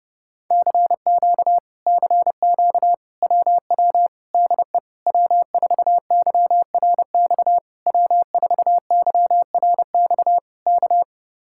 Morse code, heard as a keyed tone, about 29 wpm, 700 Hz. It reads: CQ CQ WW DE W4YRX W4YRX K